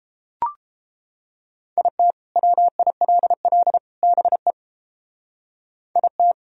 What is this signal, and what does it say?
Morse code, heard as a keyed tone, about 30 wpm, 700 Hz. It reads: IT WILL BE IT